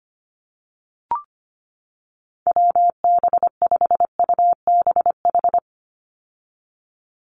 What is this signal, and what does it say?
Morse code, heard as a keyed tone, about 25 wpm, 700 Hz. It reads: WB5UBH
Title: USA Call Signs 25wpm